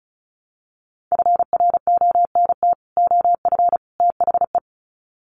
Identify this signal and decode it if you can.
Morse code, heard as a keyed tone, about 34 wpm, 700 Hz. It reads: FRONT OF THE
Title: Sets of 3 Words 35wpm